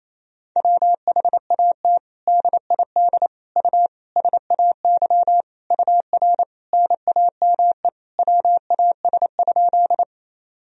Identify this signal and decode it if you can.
Morse code, heard as a keyed tone, about 28 wpm, 700 Hz. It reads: WHAT DID U SAY UR NAME WAS?
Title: Sentences from Top 200 Words and Abbreviations 28wpm